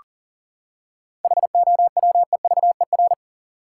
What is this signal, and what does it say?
Morse code, heard as a keyed tone, about 40 wpm, 700 Hz. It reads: HOWEVER
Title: Seven Letter Words - 40wpm